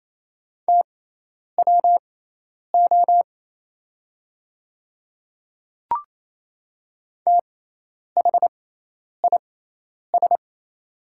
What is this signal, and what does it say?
Morse code, heard as a keyed tone, about 27 wpm, spaced as for 10 wpm, 700 Hz. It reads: TWO THIS
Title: 6x Character Spacing - Top 100 Words - 28wpm